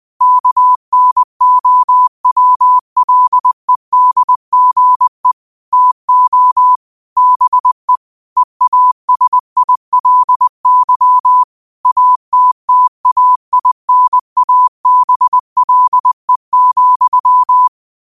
Morse code, heard as a keyed tone, about 20 wpm, 1000 Hz. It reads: KNOWLEDGE TO BE EASILY ATTAINABLE,
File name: qotdfiles / 20wpm